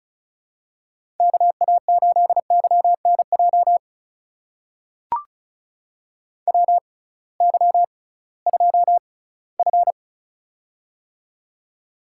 Morse code, heard as a keyed tone, about 36 wpm, spaced as for 21 wpm, 700 Hz. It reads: KA8YNJ WY2F
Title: USA Call Signs - 6x Character Spacing - 35wpm